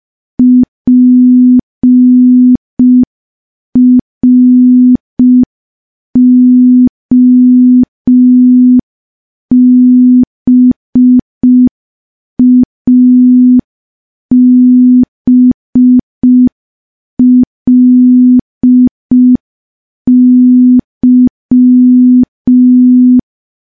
Morse code, heard as a keyed tone, about 5 wpm, 250 Hz. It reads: PROBABLY
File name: morse / morse-training-05WPM